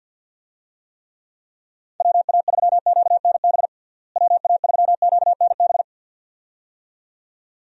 Morse code, heard as a keyed tone, about 47 wpm, 700 Hz. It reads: WA3QNB WA3QNB
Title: USA Call Signs - 6 Characters - Repeated 2x - 50wpm